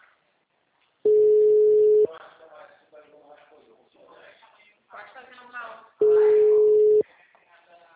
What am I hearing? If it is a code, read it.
Ringback tone (Europe, 425 Hz), with speech.
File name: out- -2051- - - .1906